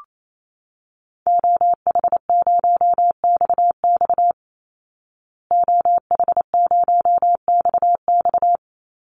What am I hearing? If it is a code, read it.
Morse code, heard as a keyed tone, about 28 wpm, 700 Hz. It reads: OH0XX OH0XX
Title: Call Signs Repeated 3x - 28wpm